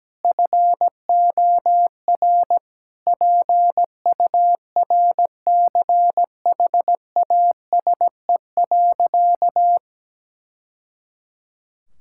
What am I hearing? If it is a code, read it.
Morse code, heard as a keyed tone, about 17 wpm, 700 Hz. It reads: FOR PURCHASE.
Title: Sentences from Top 1000 Words - Review Part 5 - 17wpm